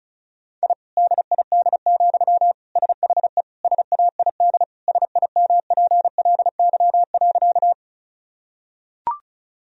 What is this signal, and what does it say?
Morse code, heard as a keyed tone, about 34 wpm, 700 Hz. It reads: I DID, SHE SAID SIMPLY.
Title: Sentences from Top 1000 Words - Review Part 4 - 35wpm